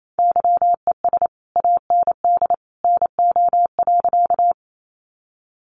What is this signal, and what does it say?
Morse code, heard as a keyed tone, about 28 wpm, 700 Hz. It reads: YES AND NO.
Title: Sentences from Top 500 Words 28wpm